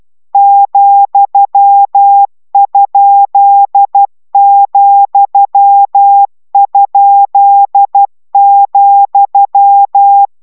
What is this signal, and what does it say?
Morse code, heard as a keyed tone, about 12 wpm, 800 Hz. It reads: ,?,?,